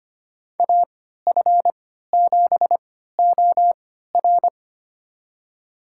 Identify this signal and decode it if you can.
Morse code, heard as a keyed tone, about 25 wpm, spaced as for 14 wpm, 700 Hz. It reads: AF7OR
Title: USA Call Signs - 5 Characters - 3x Character Spacing - 25wpm